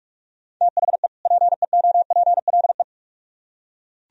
Morse code, heard as a keyed tone, about 44 wpm, 700 Hz. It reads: THE PEOPLE